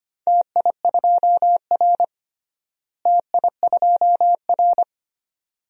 Morse code, heard as a keyed tone, about 25 wpm, 700 Hz. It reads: TI2R TI2R
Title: DXCC-100 Call Signs - 4 Characters - Repeated 2x - 25wpm